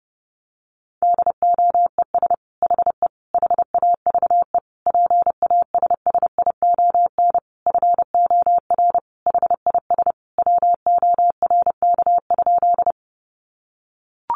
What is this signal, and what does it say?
Morse code, heard as a keyed tone, about 30 wpm, 700 Hz. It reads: DOES HE HAVE PASSION FOR HIS WORK?
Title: Sentences from Top 1300 Words 30wpm